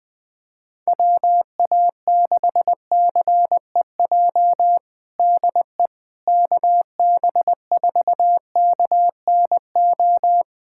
Morse code, heard as a keyed tone, about 20 wpm, 700 Hz. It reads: WA6CEJ DE KB4KNO
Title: USA Callsign DE Callsign - 6 Characters - 20wpm